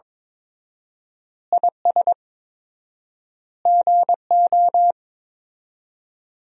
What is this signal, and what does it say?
Morse code, heard as a keyed tone, about 22 wpm, 700 Hz. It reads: IS GO
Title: 4x Word-Spacing - Sentences from Top 200 Words - 22wpm